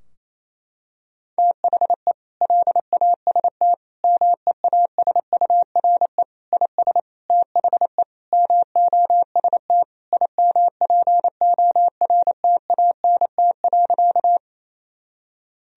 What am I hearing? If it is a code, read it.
Morse code, heard as a keyed tone, about 28 wpm, 700 Hz. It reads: THE LAST MEASURE IS THE MOST IMPORTANT.
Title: Sentences from Top 1000 Words - Review Part 2 - 28wpm